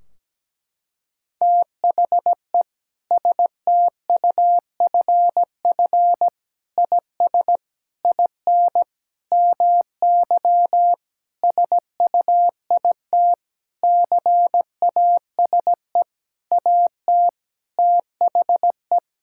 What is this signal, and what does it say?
Morse code, heard as a keyed tone, about 17 wpm, 700 Hz. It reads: THE STUFF IS IN MY SUIT CASE AT THE
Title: Sentences from Top 1600 Words 17wpm